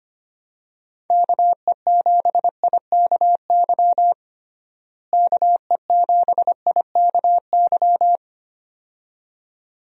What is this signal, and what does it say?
Morse code, heard as a keyed tone, about 25 wpm, 700 Hz. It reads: KE7IKY KE7IKY
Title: USA Call Signs - 6 Characters - Repeated 2x - 25wpm